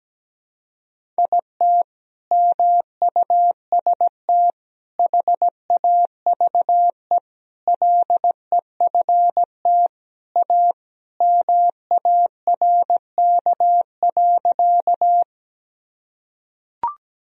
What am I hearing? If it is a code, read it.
Morse code, heard as a keyed tone, about 17 wpm, 700 Hz. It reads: IT MUST HAVE LEFT A MARK.
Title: Sentences from Top 1000 Words - Review Part 1 - 17wpm